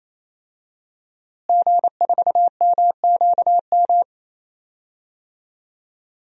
Morse code, heard as a keyed tone, about 28 wpm, 700 Hz. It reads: G4MQM
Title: DXCC-100 Call Signs - 5 Characters - Repeated 3x - 28wpm